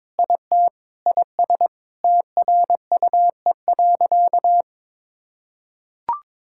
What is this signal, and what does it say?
Morse code, heard as a keyed tone, about 22 wpm, 700 Hz. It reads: IT IS TRUE.
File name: practice / 22wpm / Sentences with 8-Word Phrases - 22wpm